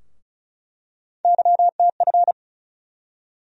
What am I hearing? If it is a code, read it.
Morse code, heard as a keyed tone, about 34 wpm, 700 Hz. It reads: YTF
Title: USA Call Sign Postfixes - 3 Letters 35wpm